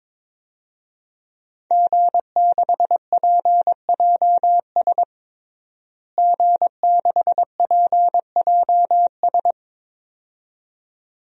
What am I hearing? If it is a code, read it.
Morse code, heard as a keyed tone, about 22 wpm, 700 Hz. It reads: G6PJS G6PJS
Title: DXCC-100 Call Signs - 5 Characters - Repeated 2x - 22wpm